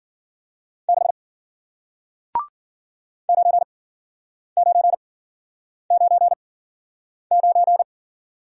Morse code, heard as a keyed tone, about 52 wpm, 700 Hz. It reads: 6 9 9 9 9